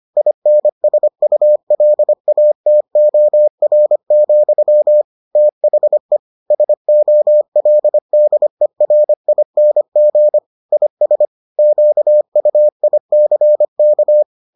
Morse code, heard as a keyed tone, about 25 wpm, 600 Hz. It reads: INSULATOR, THE SOLDERING IS QUICK